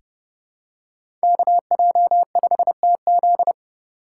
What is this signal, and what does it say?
Morse code, heard as a keyed tone, about 30 wpm, 700 Hz. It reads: KJ5TZ
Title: USA Call Signs - 5 Characters - 4x Character Spacing - 30wpm